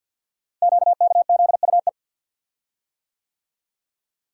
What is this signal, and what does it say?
Morse code, heard as a keyed tone, about 50 wpm, 690 Hz. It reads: QKZUE
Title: Code Groups - 5 Letters - 50wpm